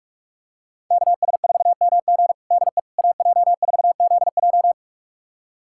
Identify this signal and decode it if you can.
Morse code, heard as a keyed tone, about 46 wpm, 700 Hz. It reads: KS4MG DE AJ4ZJ
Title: USA Callsign DE Callsign - 5 Characters - 45wpm